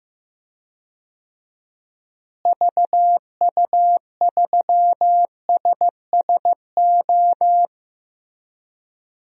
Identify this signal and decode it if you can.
Morse code, heard as a keyed tone, about 15 wpm, 700 Hz. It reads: VU3SSO
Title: DXCC-100 Call Signs - 6 Characters - Repeated 2x - 15wpm